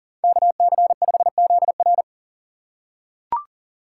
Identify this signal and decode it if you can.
Morse code, heard as a keyed tone, about 40 wpm, 700 Hz. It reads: KC5ZR
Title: USA Call Signs - 5 Characters - 2x Character Spacing - 40wpm